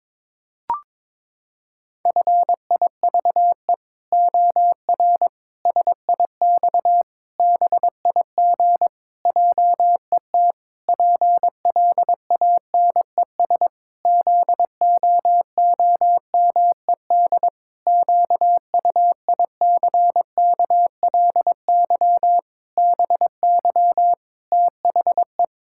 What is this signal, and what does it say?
Morse code, heard as a keyed tone, about 22 wpm, 700 Hz. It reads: FIVE OR SIX BIG JET PLANES ZOOMED QUICKLY BY THE
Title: Pangram Challenge v2 - 22wpm